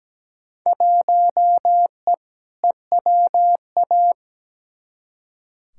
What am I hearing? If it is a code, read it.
Morse code, heard as a keyed tone, about 17 wpm, 700 Hz. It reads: 1E EWA
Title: Field Day Exchanges - 17wpm